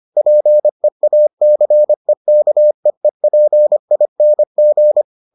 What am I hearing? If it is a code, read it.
Morse code, heard as a keyed tone, about 25 wpm, 600 Hz. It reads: PEACEKEEPING